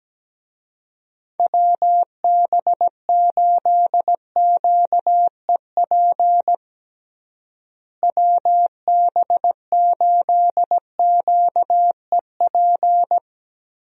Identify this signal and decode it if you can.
Morse code, heard as a keyed tone, about 17 wpm, 700 Hz. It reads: WB8QEP WB8QEP
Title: USA Call Signs - 6 Characters - Repeated 2x - 17wpm